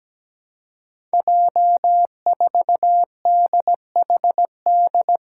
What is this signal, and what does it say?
Morse code, heard as a keyed tone, about 17 wpm, 700 Hz. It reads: J4DHD
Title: Five Challenging Letter-Number Combinations 17wpm